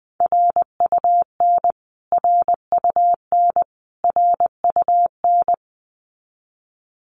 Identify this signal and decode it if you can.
Morse code, heard as a keyed tone, about 20 wpm, 700 Hz. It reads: RUN RUN RUN
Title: Top 300 Words Repeated 20wpm